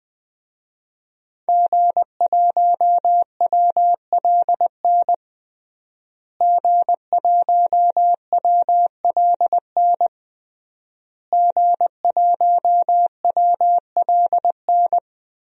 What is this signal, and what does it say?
Morse code, heard as a keyed tone, about 20 wpm, 700 Hz. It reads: G1WLN G1WLN G1WLN
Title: DXCC-100 Call Signs - 5 Characters - Repeated 3x - 20wpm